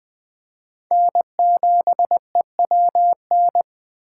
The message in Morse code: N7EWN

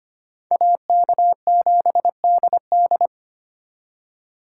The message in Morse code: AK7DD

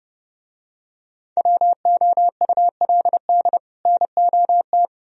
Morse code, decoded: WOULD NOT

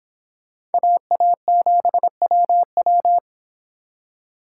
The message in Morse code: AA7WW

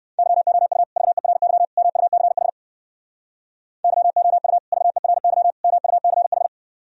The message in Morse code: YOU VERY MUCH YOU VERY MUCH

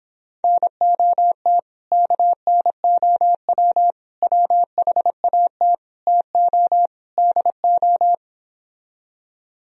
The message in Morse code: NOT KNOW WHAT TO DO

